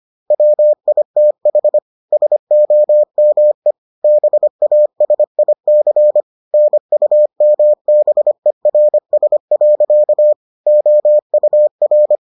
Morse code, heard as a keyed tone, about 25 wpm, 600 Hz. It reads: WITH SOME BASIC NUMBERS. OUR